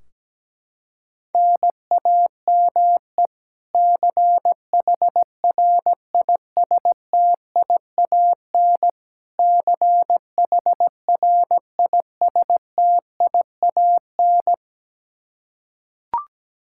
Morse code, heard as a keyed tone, about 17 wpm, 700 Hz. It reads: NAME CHRISTIAN CHRISTIAN
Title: Practice QSO - Exchange 1 - Beginning - 17wpm